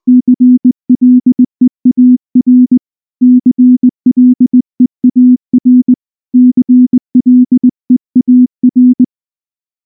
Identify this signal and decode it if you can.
Morse code, heard as a keyed tone, about 20 wpm, 250 Hz. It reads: CLEAR CLEAR CLEAR
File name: morse / morse-training-20WPM-2